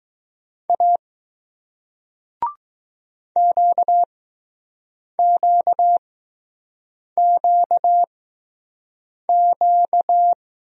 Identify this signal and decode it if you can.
Morse code, heard as a keyed tone, about 20 wpm, spaced as for 6 wpm, 700 Hz. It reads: A QQQQ